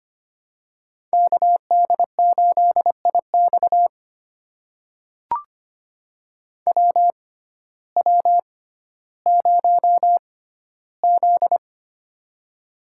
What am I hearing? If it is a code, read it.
Morse code, heard as a keyed tone, about 25 wpm, spaced as for 14 wpm, 700 Hz. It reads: KD8IX WW0Z
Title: USA Call Signs - 6x Character Spacing - 25wpm